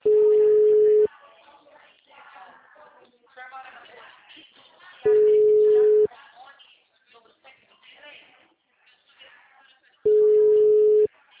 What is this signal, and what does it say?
Ringback tone (Europe, 425 Hz), with speech.
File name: out- -2012- - - .66557